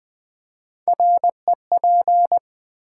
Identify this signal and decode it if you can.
Morse code, heard as a keyed tone, about 21 wpm, 700 Hz. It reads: REP
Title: Build A Word - 3 Letters At A Time - 8-Letter Words - 20wpm